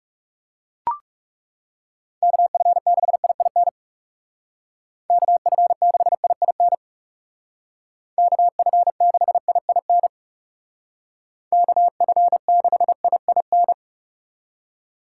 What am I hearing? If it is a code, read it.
Morse code, heard as a keyed tone, about 37 wpm, 700 Hz. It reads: KF6IIN KF6IIN KF6IIN KF6IIN
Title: Speed Racing - USA Call Signs 30wpm